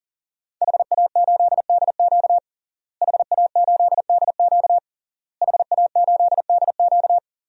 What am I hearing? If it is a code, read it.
Morse code, heard as a keyed tone, about 37 wpm, 700 Hz. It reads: HA8DQ HA8DQ HA8DQ
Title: DXCC-100 Call Signs - 5 Characters - Repeated 3x - 40wpm